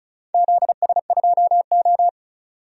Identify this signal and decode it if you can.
Morse code, heard as a keyed tone, about 34 wpm, 700 Hz. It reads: ZS2O
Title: DXCC-100 Call Signs - 4 Characters - 35wpm